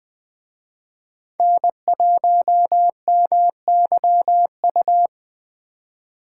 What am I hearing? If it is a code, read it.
Morse code, heard as a keyed tone, about 20 wpm, 700 Hz. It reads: N1MYU
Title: Speed Racing - USA 1x3 Call Signs 17wpm